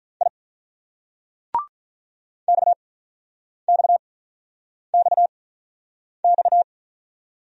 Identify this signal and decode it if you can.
Morse code, heard as a keyed tone, about 44 wpm, spaced as for 9 wpm, 700 Hz. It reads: I XXXX